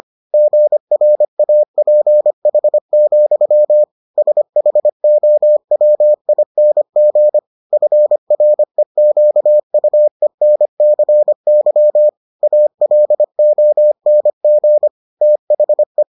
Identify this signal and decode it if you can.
Morse code, heard as a keyed tone, about 25 wpm, 600 Hz. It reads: GRAPH, SHOWING FREQUENCY ALONG THE